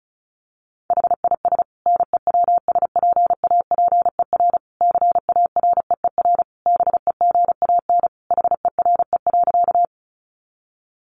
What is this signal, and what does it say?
Morse code, heard as a keyed tone, about 34 wpm, 700 Hz. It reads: HIS NEWSPAPER CAREER BEGAN HERE.
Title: Sentences from Top 1100 Words 35wpm